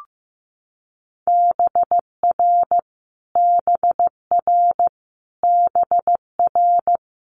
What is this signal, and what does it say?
Morse code, heard as a keyed tone, about 15 wpm, 700 Hz. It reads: BR BR BR
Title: Multi-letter Phonemes Repeated 3x 15wpm